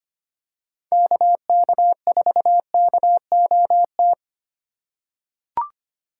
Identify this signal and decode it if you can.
Morse code, heard as a keyed tone, about 25 wpm, 700 Hz. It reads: KK4KOT
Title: USA Call Signs - 2x Character Spacing - 25wpm